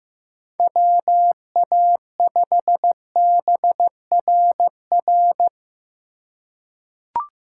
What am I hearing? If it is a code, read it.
Morse code, heard as a keyed tone, about 15 wpm, 700 Hz. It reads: WA5BRR
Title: USA Call Signs - 6 Characters - 2x Character Spacing - 15wpm